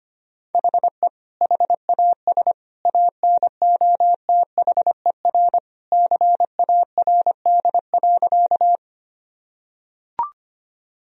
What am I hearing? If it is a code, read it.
Morse code, heard as a keyed tone, about 25 wpm, 700 Hz. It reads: HE HAS ANOTHER CARD.
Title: Sentences from Top 1000 Words - Review Part 4 - 25wpm